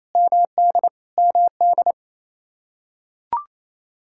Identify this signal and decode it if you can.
Morse code, heard as a keyed tone, about 28 wpm, 700 Hz. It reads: MD MD
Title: Practice QSO - Exchange 1 - Beginning - 28wpm